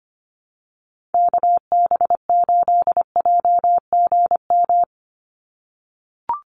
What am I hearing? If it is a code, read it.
Morse code, heard as a keyed tone, about 25 wpm, 690 Hz. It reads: KB8JGM
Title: USA Call Signs 25wpm